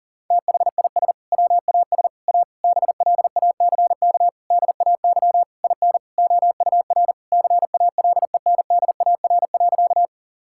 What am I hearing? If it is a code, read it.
Morse code, heard as a keyed tone, about 40 wpm, 700 Hz. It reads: THIS WAS A BLACK DAY IN OUR CALENDAR.
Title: Sentences from Top 1600 Words 40wpm